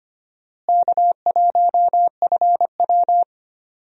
Morse code, heard as a keyed tone, about 25 wpm, 700 Hz. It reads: K1FW
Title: USA Call Signs - 4 Characters - Repeated 3x - 25wpm